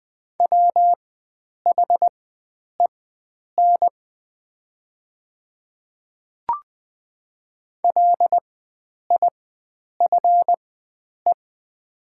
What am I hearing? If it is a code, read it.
Morse code, heard as a keyed tone, about 20 wpm, spaced as for 9 wpm, 700 Hz. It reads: WHEN LIFE
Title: 4x Character Spacing - Top 500 Words - 20wpm